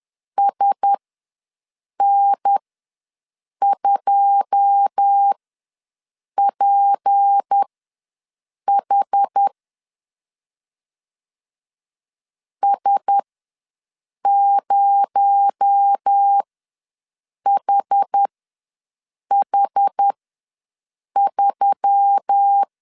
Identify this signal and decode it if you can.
Morse code, heard as a keyed tone, about 10 wpm, spaced as for 6 wpm, 800 Hz. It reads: SN2PH S0HH3
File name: LEZIONI / 9 28 ws 10wpm 4124